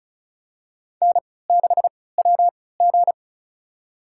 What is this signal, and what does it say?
Morse code, heard as a keyed tone, about 36 wpm, spaced as for 20 wpm, 700 Hz. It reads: N6WG
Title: USA Call Signs - 4 Characters - 3x Character Spacing - 35wpm